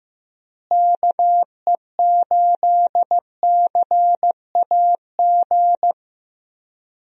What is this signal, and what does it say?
Morse code, heard as a keyed tone, about 15 wpm, 700 Hz. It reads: KE8CAG
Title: USA Call Signs - 6 Characters - Repeated 2x - 15wpm